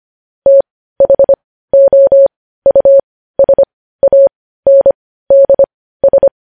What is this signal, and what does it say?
Morse code, heard as a keyed tone, about 25 wpm, spaced as for 15 wpm, 550 Hz. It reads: THOUSANDS
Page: News Headlines in Morse Code at 15 WPM